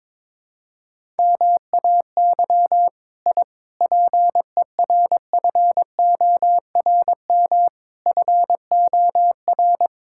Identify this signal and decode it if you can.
Morse code, heard as a keyed tone, about 21 wpm, 700 Hz. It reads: MAY I PERFORM FOR
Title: Sentences from Top 1000 Words - Review Part 5 - 22wpm